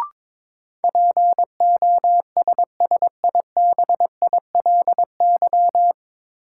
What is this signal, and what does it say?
Morse code, heard as a keyed tone, about 22 wpm, 700 Hz. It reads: POSSIBILY
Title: Sentences from Top 1300 Words - New Word Drill Alt - 22wpm